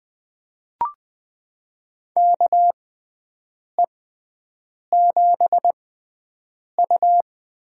Morse code, heard as a keyed tone, about 20 wpm, spaced as for 7 wpm, 700 Hz. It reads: KE7U
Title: USA Call Signs - 6x Character Spacing - 20wpm